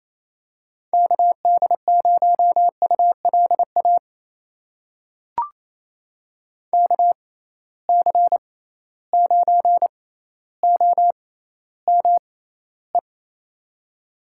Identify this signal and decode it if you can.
Morse code, heard as a keyed tone, about 28 wpm, spaced as for 14 wpm, 700 Hz. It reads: KD0ULA KC9OME
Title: USA Call Signs - 6 Characters - 6x Character Spacing - 28wpm